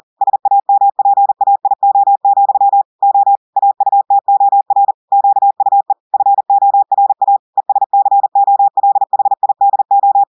Morse code, heard as a keyed tone, about 40 wpm, 800 Hz. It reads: SAMPAIO, O AUTOR QUE FORA ESCOLHIDO
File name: audios / portugais / portugais texte en clair 1 vitesse 40 mots minutes